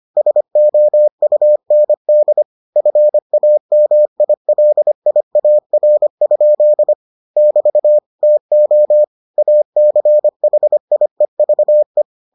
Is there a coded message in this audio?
Yes: Morse code from 0.2 to 12.0 s, heard as a keyed tone, about 25 wpm, 600 Hz. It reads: SOUND FAMILIAR? <BT> TO ACHIEVE